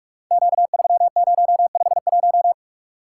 Morse code, heard as a keyed tone, about 44 wpm, 700 Hz. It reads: Q3051
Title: Code Groups - 5 Letter:Numbers - 45wpm